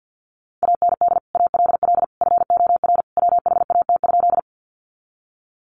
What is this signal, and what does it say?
Morse code, heard as a keyed tone, about 50 wpm, 700 Hz. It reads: AND ALL FOR WHAT?
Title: Sentences from Top 100 Words 50wpm